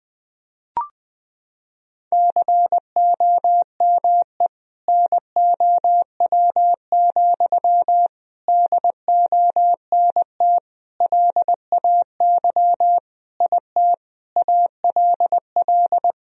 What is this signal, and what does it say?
Morse code, heard as a keyed tone, about 20 wpm, 700 Hz. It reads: COME NOW, DONT LAY IT ALL
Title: Sentences from Top 1000 Words - Review Part 2 - 20wpm